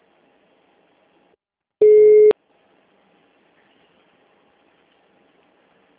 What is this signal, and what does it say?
Ringback tone (Europe, 425 Hz).